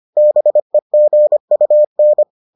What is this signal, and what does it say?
Morse code, heard as a keyed tone, about 25 wpm, 600 Hz. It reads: BEGUN